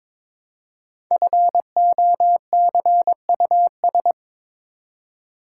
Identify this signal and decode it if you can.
Morse code, heard as a keyed tone, about 22 wpm, 690 Hz. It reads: FOCUS
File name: 5 Letter Words - First 3 Letters Given - 22wpm